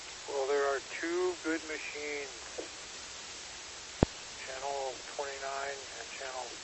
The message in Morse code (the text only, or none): none